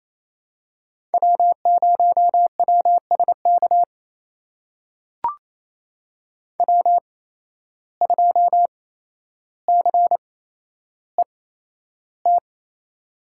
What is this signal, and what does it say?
Morse code, heard as a keyed tone, about 29 wpm, spaced as for 12 wpm, 700 Hz. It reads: W0WSK W2CET
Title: USA Call Signs - 5 Characters - 8x Character Spacing - 28wpm